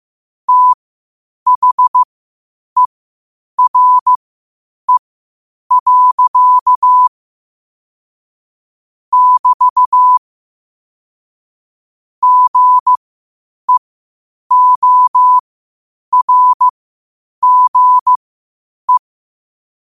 Morse code, heard as a keyed tone, about 15 wpm, spaced as for 8 wpm, 1000 Hz. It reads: THERE. <BT> GEORGE
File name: qotdfiles / 8wpm